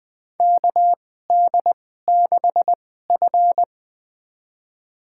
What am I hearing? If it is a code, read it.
Morse code, heard as a keyed tone, about 20 wpm, spaced as for 15 wpm, 700 Hz. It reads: KD6F
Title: USA Call Signs - 4 Characters - 2x Character Spacing - 20wpm